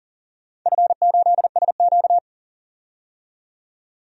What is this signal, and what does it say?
Morse code, heard as a keyed tone, about 38 wpm, 700 Hz. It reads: F8SQ
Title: DXCC-100 Call Signs - 4 Characters - 40wpm